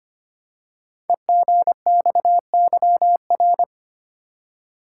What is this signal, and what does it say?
Morse code, heard as a keyed tone, about 25 wpm, 700 Hz. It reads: EGXYR
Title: Code Groups - 5 Letters - 25wpm